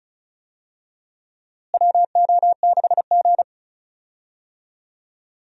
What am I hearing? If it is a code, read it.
Morse code, heard as a keyed tone, about 36 wpm, 700 Hz. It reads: WO6G
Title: USA Call Signs - 4 Characters - Repeated 3x - 35wpm